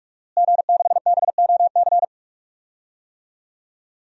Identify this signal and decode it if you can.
Morse code, heard as a keyed tone, about 46 wpm, 700 Hz. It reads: G6BOC
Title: DXCC-100 Call Signs - 5 Characters - Repeated 3x - 45wpm